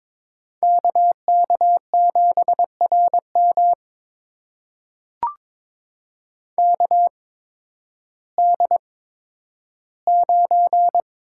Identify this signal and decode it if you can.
Morse code, heard as a keyed tone, about 22 wpm, spaced as for 12 wpm, 700 Hz. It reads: KK7RM KD9
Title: USA Call Signs - 8x Character Spacing - 22wpm